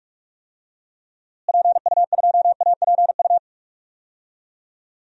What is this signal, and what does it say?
Morse code, heard as a keyed tone, about 43 wpm, 700 Hz. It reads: PU2APU
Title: DXCC-100 Call Signs - 6 Characters - Repeated 2x - 45wpm